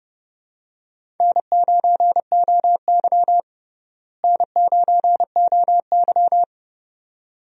Morse code, heard as a keyed tone, about 29 wpm, 700 Hz. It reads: N9OY N9OY
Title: USA Call Signs - 4 Characters - Repeated 2x - 30wpm